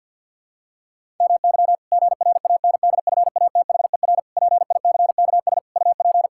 Morse code, heard as a keyed tone, about 50 wpm, 700 Hz. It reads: MY GRANDFATHER PICKS UP
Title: Pangram Challenge v2 - 50wpm